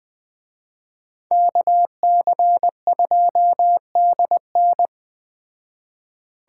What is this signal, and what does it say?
Morse code, heard as a keyed tone, about 20 wpm, 700 Hz. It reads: KC2DN